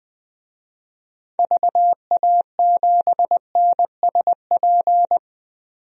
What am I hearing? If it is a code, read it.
Morse code, heard as a keyed tone, about 20 wpm, 700 Hz. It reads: VA7NSP
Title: DXCC-100 Call Signs - 6 Characters - 20wpm